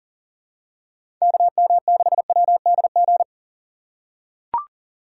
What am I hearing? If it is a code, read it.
Morse code, heard as a keyed tone, about 40 wpm, 700 Hz. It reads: KM6WDG
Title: USA Call Signs - 6 Characters - 6x Character Spacing - 40wpm